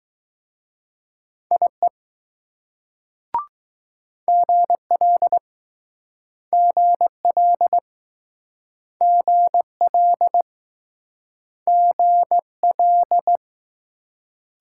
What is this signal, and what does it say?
Morse code, heard as a keyed tone, about 20 wpm, 700 Hz. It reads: IE GL GL GL GL